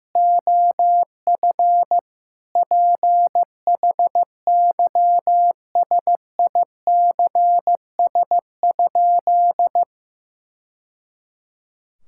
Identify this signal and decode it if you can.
Morse code, heard as a keyed tone, about 16 wpm, 700 Hz. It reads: OF PHYSICS?
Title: Sentences from Top 1000 Words - Review Part 5 - 15wpm